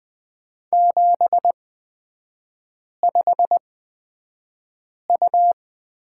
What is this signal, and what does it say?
Morse code, heard as a keyed tone, about 20 wpm, spaced as for 5 wpm, 700 Hz. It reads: 75U
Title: 250 Lesson 40e - BK - Three Characters 20wpm